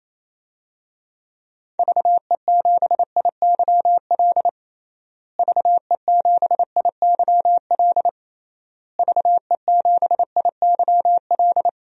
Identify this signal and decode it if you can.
Morse code, heard as a keyed tone, about 28 wpm, 700 Hz. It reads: VE7IYL VE7IYL VE7IYL
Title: DXCC-100 Call Signs - 6 Characters - Repeated 3x - 28wpm